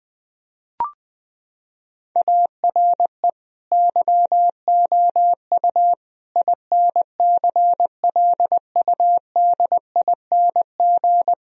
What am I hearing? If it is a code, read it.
Morse code, heard as a keyed tone, about 20 wpm, 700 Hz. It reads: ARE YOU INCLUDING
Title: Sentences from Top 1000 Words - Review Part 4 - 20wpm